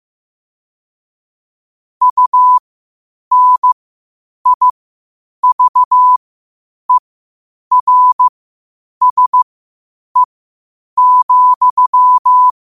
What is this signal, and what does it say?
Morse code, heard as a keyed tone, about 15 wpm, spaced as for 8 wpm, 1000 Hz. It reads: UNIVERSE,
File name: qotdfiles / 8wpm